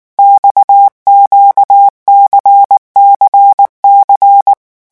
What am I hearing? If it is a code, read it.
Morse code, heard as a keyed tone, about 19 wpm, 800 Hz. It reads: XQCCC
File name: morse mp3 / revision2-vitesse 18 mots